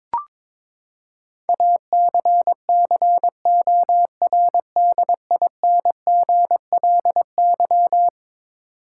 Morse code, heard as a keyed tone, about 22 wpm, 700 Hz. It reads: ACCORDINGLY
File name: practice / 22wpm / Eleven Letter Words - 22wpm